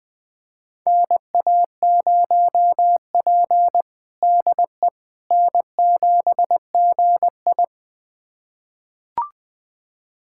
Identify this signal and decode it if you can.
Morse code, heard as a keyed tone, about 21 wpm, 700 Hz. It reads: NA0P DE N7GI
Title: USA Callsign DE Callsign - 4 Characters - 20wpm